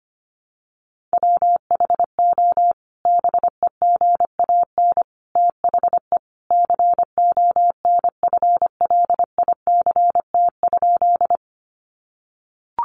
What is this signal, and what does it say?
Morse code, heard as a keyed tone, about 25 wpm, 700 Hz. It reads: WHO BEGAN THE CONFLICT?
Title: Sentences from Top 1300 Words 25wpm